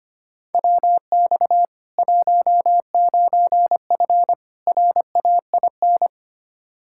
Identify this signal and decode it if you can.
Morse code, heard as a keyed tone, about 25 wpm, 700 Hz. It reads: WX 19F RAIN
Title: K1USN SST QSO Practice 25wpm